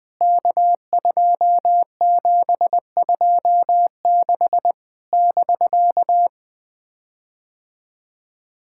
Morse code, heard as a keyed tone, about 20 wpm, 700 Hz. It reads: K2726 <BK>
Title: POTA QSO Practice - 20wpm